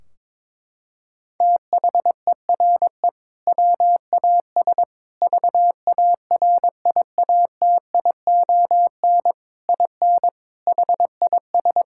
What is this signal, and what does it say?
Morse code, heard as a keyed tone, about 21 wpm, 700 Hz. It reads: THERE WAS VARIATION IN HIS
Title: Sentences from Top 1600 Words 22wpm